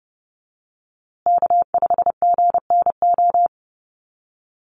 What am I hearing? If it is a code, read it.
Morse code, heard as a keyed tone, about 30 wpm, 700 Hz. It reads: K5GNO
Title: USA Call Signs Repeated 2x 30wpm